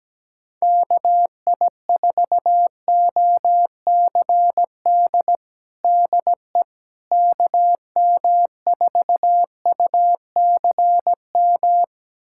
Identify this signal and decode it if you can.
Morse code, heard as a keyed tone, about 17 wpm, 700 Hz. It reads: KI4OCD DE KM4UCM
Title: USA Callsign DE Callsign - 6 Characters - 17wpm